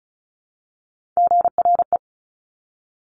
Morse code, heard as a keyed tone, about 36 wpm, 700 Hz. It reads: GRE
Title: Top Three Letter Combinations in Words 35wpm